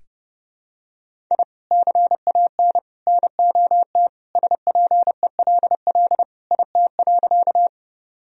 Morse code, heard as a keyed tone, about 30 wpm, 700 Hz. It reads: I CAN NOT SPELL IT.
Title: Sentences from Top 1000 Words - Review Part 1 - 30wpm